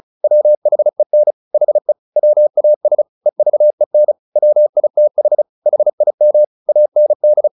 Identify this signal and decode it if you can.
Morse code, heard as a keyed tone, about 34 wpm, 600 Hz. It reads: WHEN HE WAS EVEN WITH HIM AND
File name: the old man and the sea 2 35wpm